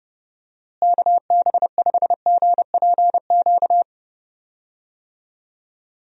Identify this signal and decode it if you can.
Morse code, heard as a keyed tone, about 29 wpm, 700 Hz. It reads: KB5GPQ
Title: USA Call Signs - 6 Characters - Repeated 3x - 30wpm